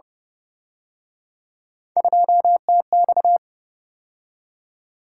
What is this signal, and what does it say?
Morse code, heard as a keyed tone, about 30 wpm, 700 Hz. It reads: 2TX